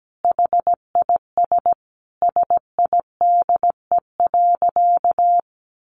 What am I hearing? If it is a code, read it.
Morse code, heard as a keyed tone, about 17 wpm, 700 Hz. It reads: HIS SIDE.
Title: Sentences from Top 400 Words 17wpm